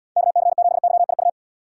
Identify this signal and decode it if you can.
Morse code, heard as a keyed tone, about 72 wpm, 700 Hz. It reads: KC7YIU